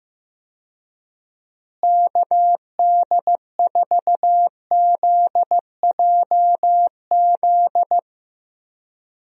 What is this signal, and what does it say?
Morse code, heard as a keyed tone, about 16 wpm, 700 Hz. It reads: KD4ZJZ
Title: USA Call Signs - 6 Characters - Repeated 2x - 15wpm